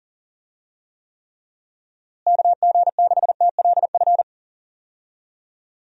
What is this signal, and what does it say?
Morse code, heard as a keyed tone, about 40 wpm, 700 Hz. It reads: KG6TLF
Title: USA Call Signs - 6 Characters - Repeated 3x - 40wpm